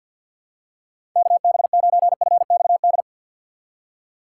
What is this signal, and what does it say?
Morse code, heard as a keyed tone, about 52 wpm, 700 Hz. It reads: KB9FXD
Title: USA Call Signs - 6 Characters - 3x Character Spacing - 50wpm